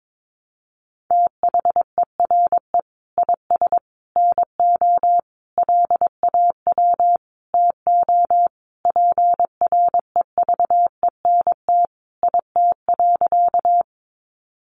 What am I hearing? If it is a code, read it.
Morse code, heard as a keyed tone, about 22 wpm, 700 Hz. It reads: THERE IS NO LAW TO PREVENT IT.
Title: Sentences from Top 1100 Words 22wpm